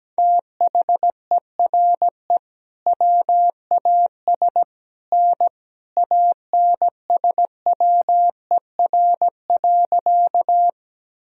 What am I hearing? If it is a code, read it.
Morse code, heard as a keyed tone, about 17 wpm, 700 Hz. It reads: THERE WAS N ANSWER.
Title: Sentences from Top 300 Words and Abbreviations 17wpm